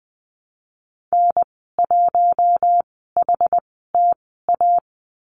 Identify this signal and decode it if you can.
Morse code, heard as a keyed tone, about 20 wpm, spaced as for 14 wpm, 700 Hz. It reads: N1HTA
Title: 161 Lesson 26g - Letter g - Call Signs 20wpm